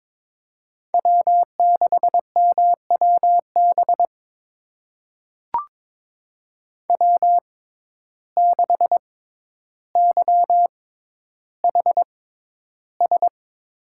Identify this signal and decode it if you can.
Morse code, heard as a keyed tone, about 22 wpm, spaced as for 11 wpm, 700 Hz. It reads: W6MWB W6YHS
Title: USA Call Signs - 5 Characters - 6x Character Spacing - 22wpm